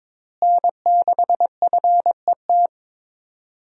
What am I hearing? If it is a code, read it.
Morse code, heard as a keyed tone, about 22 wpm, 700 Hz. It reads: N6FET
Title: USA Call Signs - 5 Characters - Repeated 2x - 22wpm